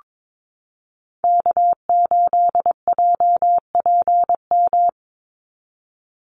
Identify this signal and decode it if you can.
Morse code, heard as a keyed tone, about 22 wpm, 700 Hz. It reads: K8JPM
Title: Call Signs Repeated 2x - 22wpm